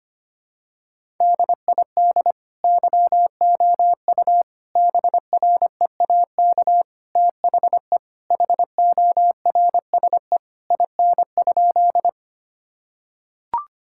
Morse code, heard as a keyed tone, about 25 wpm, 700 Hz. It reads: DID YOU BREAK THE HORSE IN?
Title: Sentences from Top 1000 Words - Review Part 6 - 25wpm